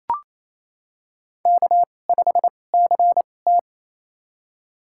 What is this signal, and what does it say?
Morse code, heard as a keyed tone, about 27 wpm, spaced as for 20 wpm, 700 Hz. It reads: K5CT
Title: USA Call Signs - 4 Characters - 2x Character Spacing - 28wpm